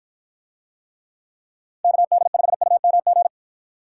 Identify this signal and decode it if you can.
Morse code, heard as a keyed tone, about 49 wpm, 700 Hz. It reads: KD5UMG